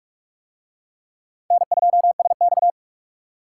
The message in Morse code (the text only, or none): N2SX